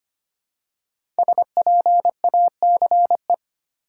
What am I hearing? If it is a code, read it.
Morse code, heard as a keyed tone, about 25 wpm, 700 Hz. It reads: SPACE
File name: 5 Letter Words - First 3 Letters Given - 25wpm